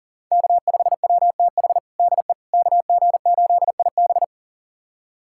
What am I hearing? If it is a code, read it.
Morse code, heard as a keyed tone, about 40 wpm, 700 Hz. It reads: K5WTH DE KG8IB
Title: USA Callsign DE Callsign - 5 Characters - 40wpm